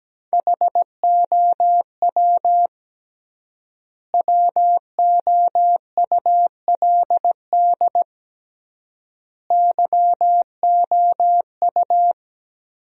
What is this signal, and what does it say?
Morse code, heard as a keyed tone, about 17 wpm, 700 Hz. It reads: HOW WOULD YOU
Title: 3x Word-Spacing - Sentences from Top 100 Words - 17wpm